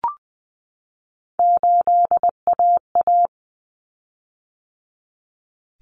Morse code, heard as a keyed tone, about 20 wpm, 700 Hz. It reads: 8AA
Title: Common Cut Numbers - 3-Digit 20wpm